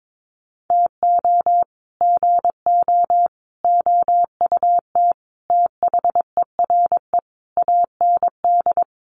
Morse code, heard as a keyed tone, about 22 wpm, 700 Hz. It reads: TO GO OUT THERE AND
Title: Sets of 5 Words Encore2 22wpm